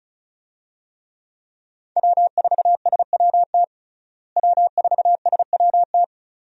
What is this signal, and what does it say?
Morse code, heard as a keyed tone, about 34 wpm, 700 Hz. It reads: W4SWT W4SWT
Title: USA Call Signs - 5 Characters - Repeated 2x - 35wpm